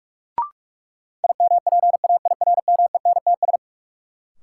Morse code, heard as a keyed tone, about 41 wpm, 700 Hz. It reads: IMPAIRMENTS
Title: Sets of 5 Words Encore - New Word Drill - 45wpm